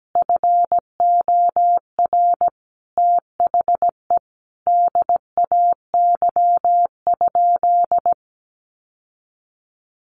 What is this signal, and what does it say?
Morse code, heard as a keyed tone, about 17 wpm, 700 Hz. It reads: FOR THE DAY?
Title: Sentences from Top 1400 Words 17wpm